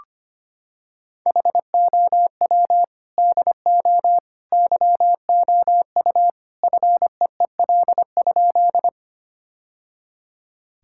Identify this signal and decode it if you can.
Morse code, heard as a keyed tone, about 25 wpm, 700 Hz. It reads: HOW DO YOU FEEL?
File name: practice / 25wpm / Sentences with 3-Word Phrases - 25wpm